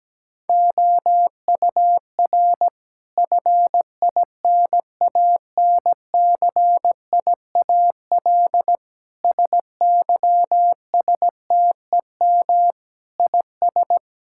Morse code, heard as a keyed tone, about 17 wpm, 700 Hz. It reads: OUR FINANCIAL SYSTEM IS
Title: Sentences from Top 1000 Words - Review Part 5 - 17wpm